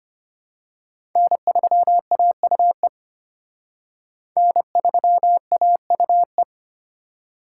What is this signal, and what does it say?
Morse code, heard as a keyed tone, about 31 wpm, 700 Hz. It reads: N3AUE N3AUE
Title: Speed Racing - USA Call Signs 25wpm